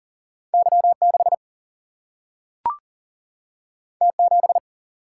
Morse code, heard as a keyed tone, about 39 wpm, 700 Hz. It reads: Y6 T7
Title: Single-Letter Single-Number 40wpm